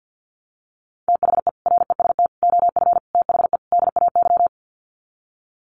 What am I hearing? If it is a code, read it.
Morse code, heard as a keyed tone, about 50 wpm, 700 Hz. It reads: THE REST OF THE DAY
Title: Sets of 5 Words 50wpm